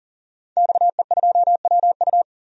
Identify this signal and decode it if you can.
Morse code, heard as a keyed tone, about 40 wpm, 700 Hz. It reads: XE2WU
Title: DXCC-100 Call Signs - 5 Characters - 40wpm